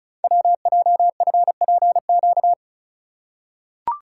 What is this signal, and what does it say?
Morse code, heard as a keyed tone, about 36 wpm, 700 Hz. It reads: WJFPQ
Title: Code Groups - 5 Letters - 35wpm